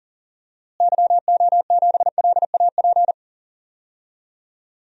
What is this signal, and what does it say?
Morse code, heard as a keyed tone, about 40 wpm, 700 Hz. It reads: YO7LAP
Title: DXCC-100 Call Signs - 6 Characters - Repeated 3x - 40wpm